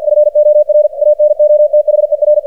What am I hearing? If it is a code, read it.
Morse code, heard as a keyed tone, about 53 wpm, 600 Hz. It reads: FOR ANOTHER
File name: archived / CRSnet